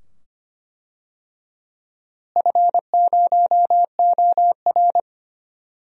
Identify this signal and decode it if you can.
Morse code, heard as a keyed tone, about 25 wpm, 700 Hz. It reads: F0OR
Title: DXCC-100 Call Signs - 4 Characters - Repeated 2x - 25wpm